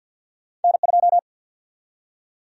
Morse code, heard as a keyed tone, about 47 wpm, 700 Hz. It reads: N2
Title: USA Call Sign Prefixes - 50wpm